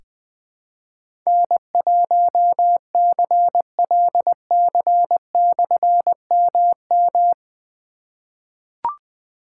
Morse code, heard as a keyed tone, about 20 wpm, 700 Hz. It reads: N1CLC/MM
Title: Difficult Call Signs 20wpm